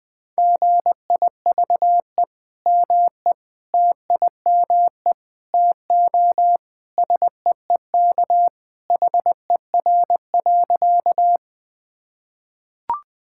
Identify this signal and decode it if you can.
Morse code, heard as a keyed tone, about 20 wpm, 700 Hz. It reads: GIVE ME TIME TO SEEK HER.
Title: Sentences from Top 1600 Words 20wpm